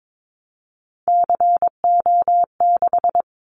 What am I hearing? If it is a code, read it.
Morse code, heard as a keyed tone, about 22 wpm, 700 Hz. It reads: CO6
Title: Callsign Prefixes for Top 100 DXCC - 22wpm